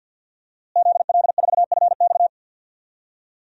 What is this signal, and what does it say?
Morse code, heard as a keyed tone, about 47 wpm, 700 Hz. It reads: ZL4FX
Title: DXCC-100 Call Signs - 5 Characters - Repeated 3x - 50wpm